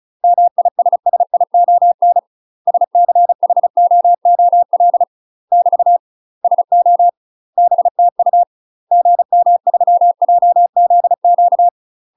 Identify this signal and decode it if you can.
Morse code, heard as a keyed tone, about 35 wpm, 700 Hz. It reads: MISSION SCHOOL <BT> SO BTU GM3JZQ
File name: UK MASTER QSOs TRAINING / 35WPM / UK MASTER QSOs 0023